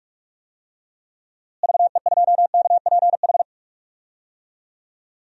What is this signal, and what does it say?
Morse code, heard as a keyed tone, about 45 wpm, 700 Hz. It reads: VE2KPH
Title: DXCC-100 Call Signs - 6 Characters - Repeated 2x - 45wpm